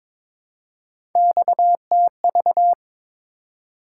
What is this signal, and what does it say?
Morse code, heard as a keyed tone, about 22 wpm, 700 Hz. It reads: XTV
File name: practice / 22wpm / Three Letter-Number Combinations - 22wpm